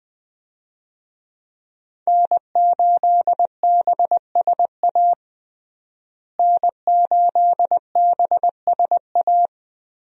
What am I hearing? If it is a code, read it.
Morse code, heard as a keyed tone, about 20 wpm, 700 Hz. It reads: N8BSA N8BSA
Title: USA Call Signs - 5 Characters - Repeated 3x - 20wpm